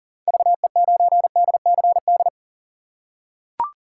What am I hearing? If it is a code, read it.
Morse code, heard as a keyed tone, about 40 wpm, 700 Hz. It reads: VE9DCD
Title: K1USN SST QSO Practice - 40wpm